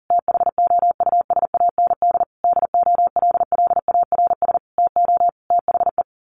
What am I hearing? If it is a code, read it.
Morse code, heard as a keyed tone, about 40 wpm, 700 Hz. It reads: THOUSAND DOLLARS TO THE
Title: A Sound Of Thunder 40wpm